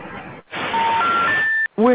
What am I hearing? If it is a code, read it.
Special information tone, with speech.